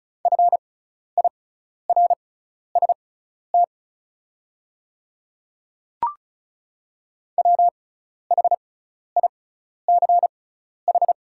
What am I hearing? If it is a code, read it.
Morse code, heard as a keyed tone, about 36 wpm, spaced as for 12 wpm, 700 Hz. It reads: FIRST WHICH
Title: 6x Character Spacing - Top 100 Words - 35wpm